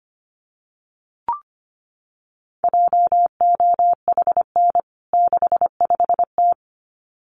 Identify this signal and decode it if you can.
Morse code, heard as a keyed tone, about 25 wpm, 700 Hz. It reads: JOHN 65T
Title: CWT Exchange 25wpm